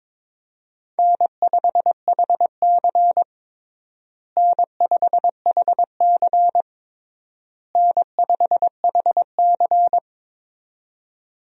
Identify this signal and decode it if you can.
Morse code, heard as a keyed tone, about 22 wpm, 700 Hz. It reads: N5HC N5HC N5HC
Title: USA Call Signs - 4 Characters - Repeated 3x - 22wpm